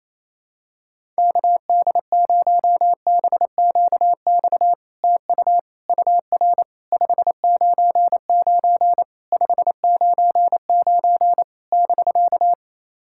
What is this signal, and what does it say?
Morse code, heard as a keyed tone, about 29 wpm, 700 Hz. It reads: KD0BQX TU UR 599 599 <BK>
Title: SOTA QSO Practice 28wpm